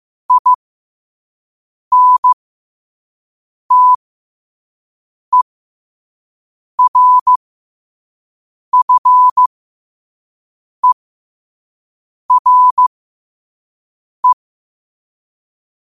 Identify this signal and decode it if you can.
Morse code, heard as a keyed tone, about 15 wpm, spaced as for 5 wpm, 1000 Hz. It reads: INTERFERE
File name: qotdfiles / 5wpm